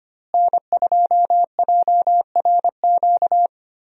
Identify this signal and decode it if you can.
Morse code, heard as a keyed tone, about 25 wpm, 700 Hz. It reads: N2JRQ
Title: USA Call Signs - 5 Characters - Repeated 3x - 25wpm